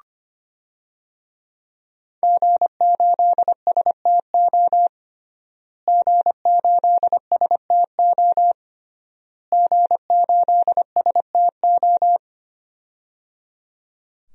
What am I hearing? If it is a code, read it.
Morse code, heard as a keyed tone, about 25 wpm, 700 Hz. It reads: G8STO G8STO G8STO
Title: DXCC-100 Call Signs - 5 Characters - Repeated 3x - 25wpm